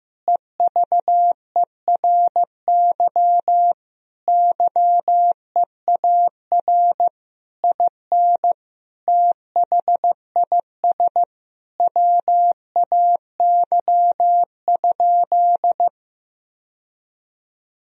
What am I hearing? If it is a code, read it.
Morse code, heard as a keyed tone, about 15 wpm, 700 Hz. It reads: EVERY YEAR IN THIS WAY?
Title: Sentences from Top 1000 Words - Review Part 1 - 15wpm